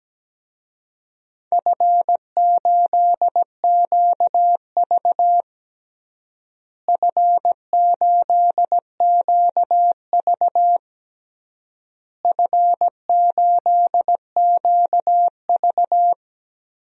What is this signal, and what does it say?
Morse code, heard as a keyed tone, about 17 wpm, 700 Hz. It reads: F8QV F8QV F8QV
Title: DXCC-100 Call Signs - 4 Characters - Repeated 3x - 17wpm